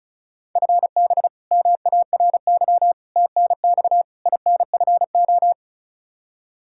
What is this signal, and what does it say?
Morse code, heard as a keyed tone, about 36 wpm, 700 Hz. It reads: FB MARY TNX INFO
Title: Practice QSO - Exchange 3 - More Info - 35wpm